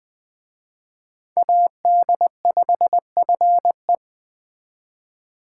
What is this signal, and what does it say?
Morse code, heard as a keyed tone, about 20 wpm, 700 Hz. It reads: AD5FE